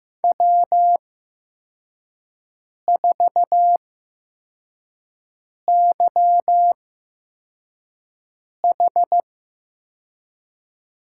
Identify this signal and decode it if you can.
Morse code, heard as a keyed tone, about 15 wpm, spaced as for 4 wpm, 700 Hz. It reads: W4YH
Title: USA Call Signs - 5 Characters - 8x Character Spacing - 15wpm